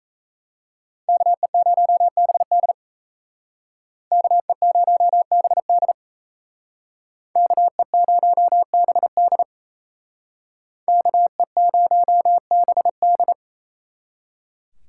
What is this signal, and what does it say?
Morse code, heard as a keyed tone, about 37 wpm, 700 Hz. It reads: KE0BD KE0BD KE0BD KE0BD